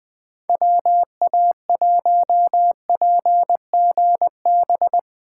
Morse code, heard as a keyed tone, about 20 wpm, 700 Hz. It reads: WA1PGB
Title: Speed Racing - USA 2x3 Call Signs 17wpm